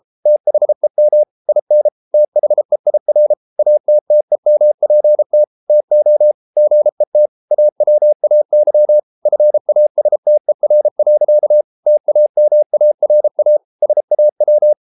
Morse code, heard as a keyed tone, about 33 wpm, 600 Hz. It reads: THEM IN THEIR ATTEMPT TO GET AWAY FASTER. TAMARA SAW